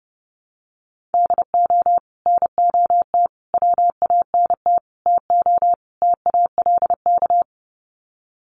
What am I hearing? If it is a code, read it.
Morse code, heard as a keyed tone, about 28 wpm, 700 Hz. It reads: DO NOT WANT TO TALK
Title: Sets of 5 Words 30wpm